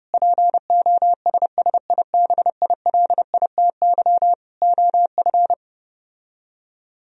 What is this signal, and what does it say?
Morse code, heard as a keyed tone, about 33 wpm, 700 Hz. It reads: POSSIBILITY OF